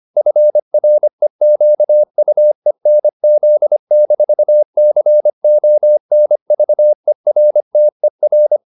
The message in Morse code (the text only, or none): FREQUENZ-CONVERTER